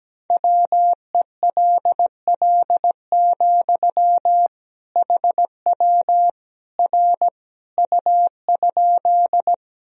Morse code, heard as a keyed tone, about 17 wpm, 700 Hz. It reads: WELL, HW R U?
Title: Sentences from Top 200 Words and Abbreviations 17wpm